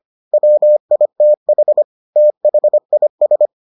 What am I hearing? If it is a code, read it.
Morse code, heard as a keyed tone, about 25 wpm, 600 Hz. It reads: WITH THIS